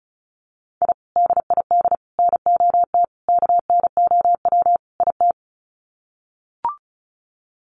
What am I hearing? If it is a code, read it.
Morse code, heard as a keyed tone, about 34 wpm, 700 Hz. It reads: I DID NOT KNOW IT
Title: Sets of 5 Words Encore2 35wpm